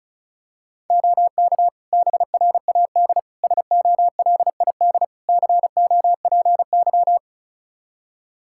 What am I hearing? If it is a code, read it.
Morse code, heard as a keyed tone, about 36 wpm, 700 Hz. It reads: OK BRAD SOLID COPY
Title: Practice QSO - Exchange 3 - More Info - 35wpm